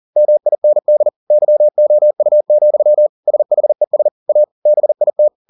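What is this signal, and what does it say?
Morse code, heard as a keyed tone, about 40 wpm, 600 Hz. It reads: MIND YOU, SHES A BIT
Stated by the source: Morse code at 40 wpm, practice text, CW med Gnister.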